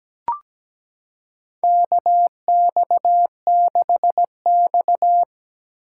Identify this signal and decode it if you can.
Morse code, heard as a keyed tone, about 17 wpm, 700 Hz. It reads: KX6X
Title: K1USN SST QSO Practice - 17wpm